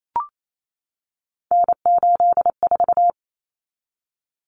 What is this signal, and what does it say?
Morse code, heard as a keyed tone, about 28 wpm, 700 Hz. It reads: N84
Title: Common Cut Numbers - 3-Digit 28wpm